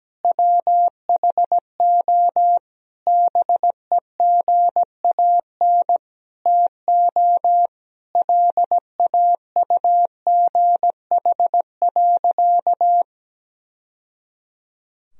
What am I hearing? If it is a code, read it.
Morse code, heard as a keyed tone, about 17 wpm, 700 Hz. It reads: WHO BEGAN TO LAUGH.
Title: Sentences from Top 1000 Words - Review Part 1 - 17wpm